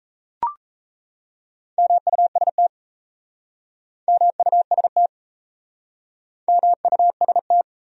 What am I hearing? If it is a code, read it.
Morse code, heard as a keyed tone, about 37 wpm, 700 Hz. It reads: MUST MUST MUST